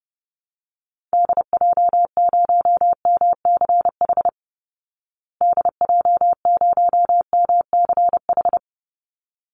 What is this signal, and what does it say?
Morse code, heard as a keyed tone, about 30 wpm, 700 Hz. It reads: DJ0MCH DJ0MCH
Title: Call Signs Repeated 3x - 30wpm